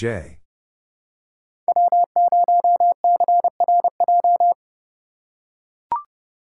Morse code, heard as a keyed tone, about 30 wpm, 700 Hz. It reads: W0CRJ
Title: USA Call Signs - 5 Characters - 2x Character Spacing - 30wpm